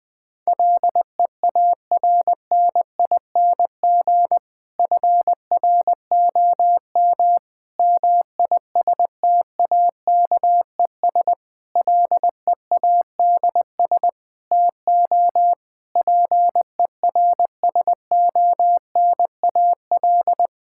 Morse code, heard as a keyed tone, about 20 wpm, 700 Hz. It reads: LEARNING FROM MISTAKES LEADS TO PERSONAL
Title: Difficult Sentences with Hints - 20wpm